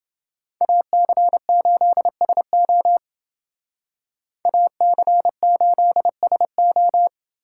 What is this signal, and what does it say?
Morse code, heard as a keyed tone, about 28 wpm, 700 Hz. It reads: AC8SO AC8SO